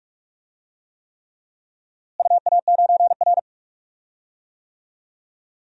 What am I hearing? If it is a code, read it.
Morse code, heard as a keyed tone, about 44 wpm, 700 Hz. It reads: UA9R
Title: DXCC-100 Call Signs - 4 Characters - Repeated 3x - 45wpm